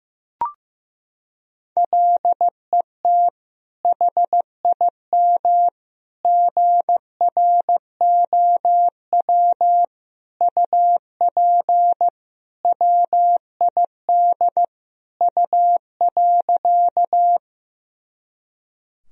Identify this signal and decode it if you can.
Morse code, heard as a keyed tone, about 15 wpm, 700 Hz. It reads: LET HIM GROW UP WID U.
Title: Sentences from Top 300 Words and Abbreviations 15wpm